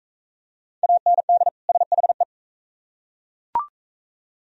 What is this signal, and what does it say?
Morse code, heard as a keyed tone, about 38 wpm, 700 Hz. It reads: AND SHE